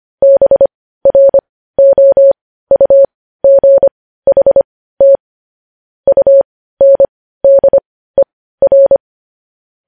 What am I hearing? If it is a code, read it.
Morse code, heard as a keyed tone, about 25 wpm, spaced as for 15 wpm, 550 Hz. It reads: BROUGHT UNDER
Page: news-headlines-in-morse-code-at-15-wpm